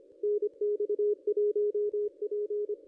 Morse code, heard as a keyed tone, about 26 wpm, 420 Hz, with radio noise. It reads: NX1P
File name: contests / ss-cw-k5tr-op-n5rz -1400